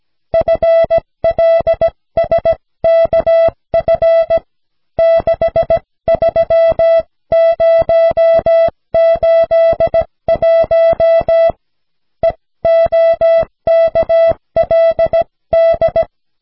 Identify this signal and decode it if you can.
Morse code, heard as a keyed tone, about 16 wpm, 660 Hz. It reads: FLSKF 63081 EOKLD
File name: cw / snelheid 4